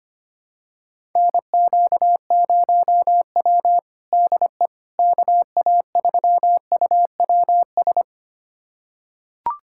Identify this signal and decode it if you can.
Morse code, heard as a keyed tone, about 25 wpm, 700 Hz. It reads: NQ0W DE KA3UWS
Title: USA Callsign DE Callsign 25wpm